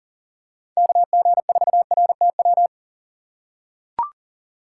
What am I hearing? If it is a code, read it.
Morse code, heard as a keyed tone, about 40 wpm, 700 Hz. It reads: KG4RTW
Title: USA Call Signs - 6 Characters - 3x Character Spacing - 40wpm